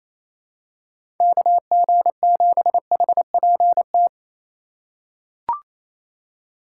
Morse code, heard as a keyed tone, about 28 wpm, 700 Hz. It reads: KG7HPT
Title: USA Call Signs - 6 Characters - 2x Character Spacing - 28wpm